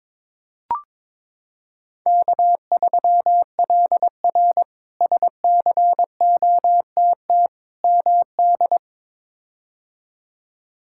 Morse code, heard as a keyed tone, about 21 wpm, 700 Hz. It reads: K3LR SCOTT MD
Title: K1USN SST QSO Practice - 22wpm